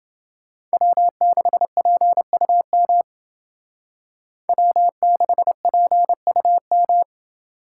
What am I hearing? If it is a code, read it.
Morse code, heard as a keyed tone, about 28 wpm, 700 Hz. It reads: W6PUM W6PUM